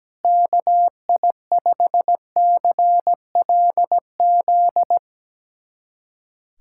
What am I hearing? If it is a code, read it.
Morse code, heard as a keyed tone, about 17 wpm, 700 Hz. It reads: KI5CLZ
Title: USA Callsign DE Callsign - 6 Characters - 17wpm